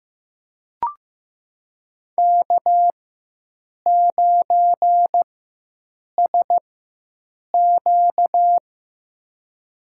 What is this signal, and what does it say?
Morse code, heard as a keyed tone, about 15 wpm, spaced as for 7 wpm, 700 Hz. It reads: K9SQ
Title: USA Call Signs - 4 Characters - 4x Character Spacing - 15wpm